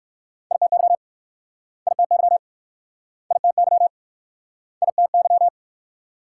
Morse code, heard as a keyed tone, about 60 wpm, 700 Hz. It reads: ITY ITY ITY ITY